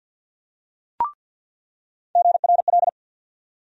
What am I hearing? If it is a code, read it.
Morse code, heard as a keyed tone, about 47 wpm, 700 Hz. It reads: GRL
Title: USA Call Sign Postfixes - 3 Letters 50wpm